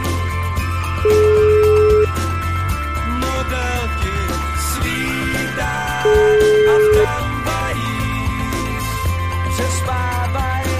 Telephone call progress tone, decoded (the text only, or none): ringback tone